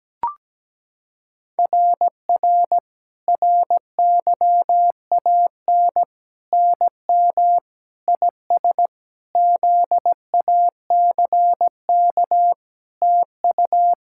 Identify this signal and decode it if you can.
Morse code, heard as a keyed tone, about 17 wpm, 700 Hz. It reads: RR RYAN NM IS ZACK TU
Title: SOTA QSO Practice 17wpm